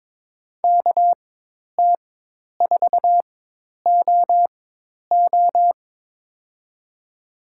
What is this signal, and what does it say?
Morse code, heard as a keyed tone, about 22 wpm, spaced as for 10 wpm, 700 Hz. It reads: KT4OO
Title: USA Call Signs - 4x Character Spacing - 22wpm